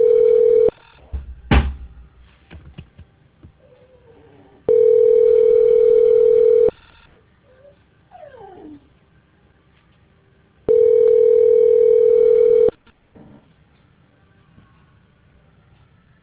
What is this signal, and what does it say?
Ringback tone (North America), 440 + 480 Hz.